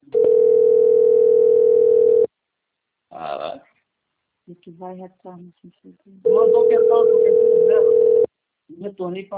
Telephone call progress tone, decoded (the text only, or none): ringback tone